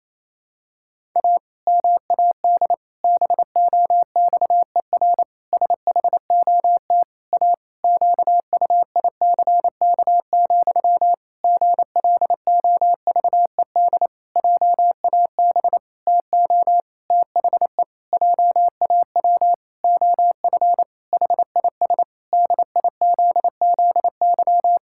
Morse code, heard as a keyed tone, about 27 wpm, 700 Hz. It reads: A MAD BOXER SHOT A QUICK, GLOVED JAB TO THE JAW OF HIS DIZZY
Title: Pangram Challenge v2 - 28wpm